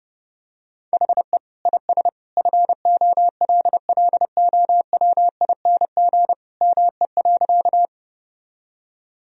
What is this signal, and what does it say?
Morse code, heard as a keyed tone, about 32 wpm, 700 Hz. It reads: HE IS FOLLOWING ME.
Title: Sentences from Top 1000 Words - Review Part 4 - 30wpm